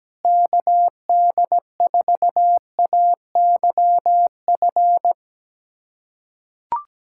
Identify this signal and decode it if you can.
Morse code, heard as a keyed tone, about 17 wpm, 700 Hz. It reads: KD4AYF
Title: USA Callsign DE Callsign - 6 Characters - 17wpm